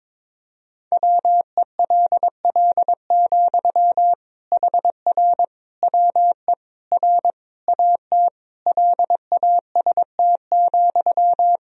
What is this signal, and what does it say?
Morse code, heard as a keyed tone, about 22 wpm, 700 Hz. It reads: WELL, HR WE R AT LAST,
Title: Sentences from Top 300 Words and Abbreviations 22wpm